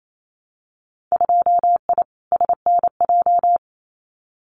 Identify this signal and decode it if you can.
Morse code, heard as a keyed tone, about 28 wpm, 700 Hz. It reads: 2I SNJ
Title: Winter Field Day Exchanges Encore 28wpm